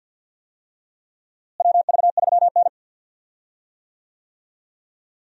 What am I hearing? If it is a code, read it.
Morse code, heard as a keyed tone, about 46 wpm, 700 Hz. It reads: WV3N
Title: USA Call Signs - 4 Characters - Repeated 3x - 50wpm